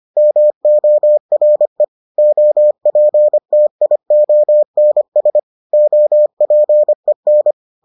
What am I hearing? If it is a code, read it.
Morse code, heard as a keyed tone, about 25 wpm, 600 Hz. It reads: MORE OPTIONS OPEN